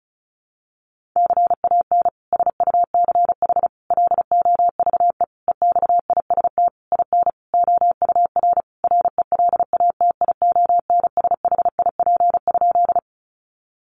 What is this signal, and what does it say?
Morse code, heard as a keyed tone, about 34 wpm, 700 Hz. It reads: CAN SUCH LOVE EXIST IN OUR RELATIONSHIP?
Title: Sentences from Top 1100 Words 35wpm